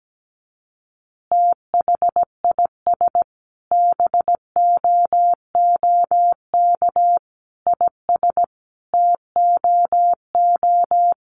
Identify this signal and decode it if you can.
Morse code, heard as a keyed tone, about 17 wpm, 700 Hz. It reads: THIS BOOK IS TOO
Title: Additional Easy Phrases 17wpm